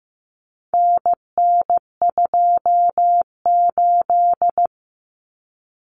Morse code, heard as a keyed tone, about 15 wpm, 700 Hz. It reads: NN28
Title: Common Cut Numbers - 4-Digit 15wpm